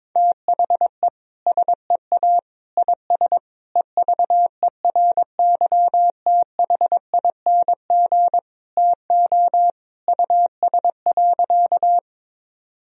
Morse code, heard as a keyed tone, about 21 wpm, 700 Hz. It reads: THE SEA IS EVERYTHING TO US.
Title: Sentences from Top 1000 Words - Review Part 1 - 22wpm